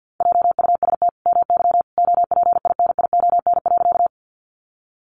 Morse code, heard as a keyed tone, about 50 wpm, 700 Hz. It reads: JUST MY OPINION.
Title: Sentences from Top 1200 Words 50wpm